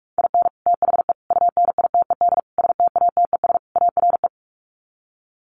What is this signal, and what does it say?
Morse code, heard as a keyed tone, about 44 wpm, 700 Hz. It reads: IN THE UNITED STATES ARE
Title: Sets of 5 Words Encore 45wpm